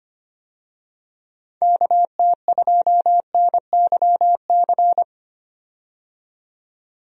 Morse code, heard as a keyed tone, about 25 wpm, 700 Hz. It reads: KT2NYC
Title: USA Call Signs - 6 Characters - 25wpm